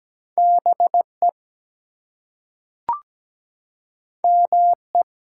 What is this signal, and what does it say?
Morse code, heard as a keyed tone, about 17 wpm, 700 Hz. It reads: BE ME
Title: Two Letter Words - 17wpm